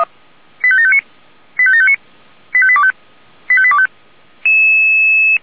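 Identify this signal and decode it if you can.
Five-tone selective call (ZVEI), heard as quick runs of five tones, tones of about 70 ms. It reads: 85759 85759 85825 85825
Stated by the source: ZVEI five-tone call 85827, received on SDR.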